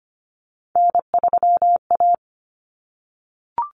Morse code, heard as a keyed tone, about 25 wpm, 700 Hz. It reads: N3A
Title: Common Cut Numbers - 3-Digit 25wpm